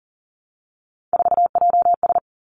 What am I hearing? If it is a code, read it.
Morse code, heard as a keyed tone, about 39 wpm, 700 Hz. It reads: 4JS
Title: Three Challenging Letter-Number Combinations 40wpm